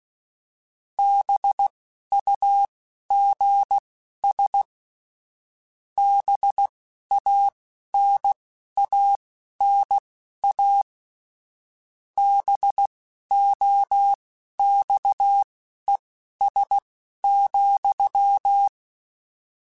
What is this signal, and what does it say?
Morse code, heard as a keyed tone, about 16 wpm, spaced as for 12 wpm, 800 Hz. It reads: BUGS BANANA BOXES,
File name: Misc / Misc-mp3s / BIG @10